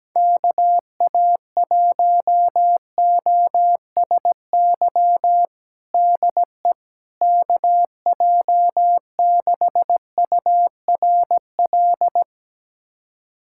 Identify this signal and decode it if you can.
Morse code, heard as a keyed tone, about 17 wpm, 700 Hz. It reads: KA1OSY DE KJ6URL
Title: USA Callsign DE Callsign - 6 Characters - 17wpm